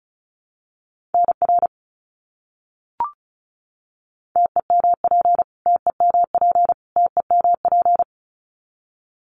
Morse code, heard as a keyed tone, about 34 wpm, 700 Hz. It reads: NR TEMP TEMP TEMP
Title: Abbreviations Repeated 3x - 35wpm